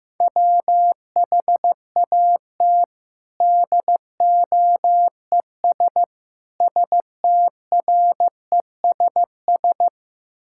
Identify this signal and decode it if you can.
Morse code, heard as a keyed tone, about 15 wpm, 700 Hz. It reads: WHAT DOES STRESS
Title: Sentences from Top 1000 Words - Review Part 6 - 15wpm